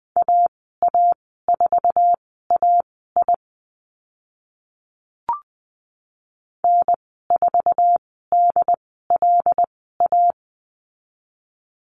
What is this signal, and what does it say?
Morse code, heard as a keyed tone, about 20 wpm, spaced as for 15 wpm, 700 Hz. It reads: AA4AI N4DLA
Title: 076 Lesson 13g - Letter L - Call Signs 20wpm